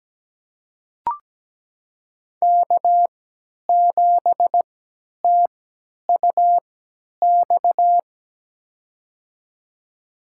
Morse code, heard as a keyed tone, about 17 wpm, spaced as for 10 wpm, 700 Hz. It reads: K7TUX
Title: USA Call Signs - 5 Characters - 3x Character Spacing - 17wpm